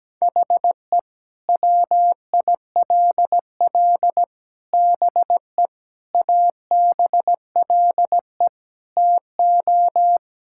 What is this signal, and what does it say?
Morse code, heard as a keyed tone, about 17 wpm, 700 Hz. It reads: HE WILL BE ABLE TO
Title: Sentences from Top 1600 Words 17wpm